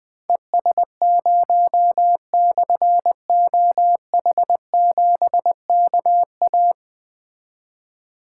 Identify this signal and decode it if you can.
Morse code, heard as a keyed tone, about 21 wpm, 700 Hz. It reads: ES0/OH7KA
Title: Difficult Call Signs 20wpm